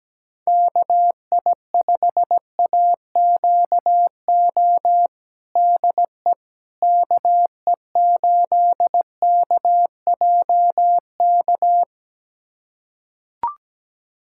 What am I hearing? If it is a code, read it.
Morse code, heard as a keyed tone, about 17 wpm, 700 Hz. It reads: KI5AQO DE KE8KJK
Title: USA Callsign DE Callsign 17wpm